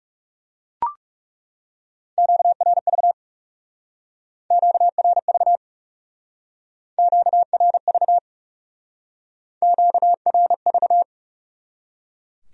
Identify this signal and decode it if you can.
Morse code, heard as a keyed tone, about 36 wpm, 700 Hz. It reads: QRV QRV QRV QRV
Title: Speed-Racing Q-Codes - 30wpm